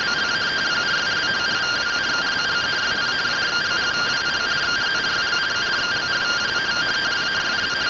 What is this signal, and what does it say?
Two-tone teleprinter signal, about 170 Hz shift.